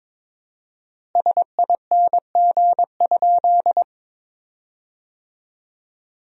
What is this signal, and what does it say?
Morse code, heard as a keyed tone, about 23 wpm, 700 Hz. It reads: SING?
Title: 4x Word-Spacing - Sentences from Top 400 Words - 22wpm